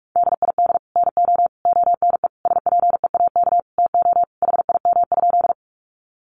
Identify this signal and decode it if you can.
Morse code, heard as a keyed tone, about 44 wpm, 700 Hz. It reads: DID NO ONE SPEAK TO HIM?
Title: Sentences from Top 1100 Words 45wpm